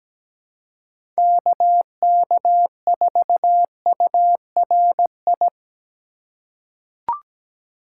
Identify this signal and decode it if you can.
Morse code, heard as a keyed tone, about 17 wpm, 700 Hz. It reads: KK4URI
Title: USA Call Signs - 4x Character Spacing - 17wpm